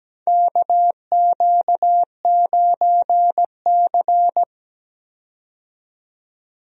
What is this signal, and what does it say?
Morse code, heard as a keyed tone, about 17 wpm, 700 Hz. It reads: KQ9C
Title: USA Call Signs - 4 Characters - 17wpm